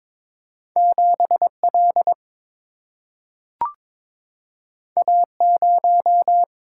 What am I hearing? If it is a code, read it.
Morse code, heard as a keyed tone, about 22 wpm, 700 Hz. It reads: 7L A0
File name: practice / 22wpm / Two Letter-Number Combinations - 22wpm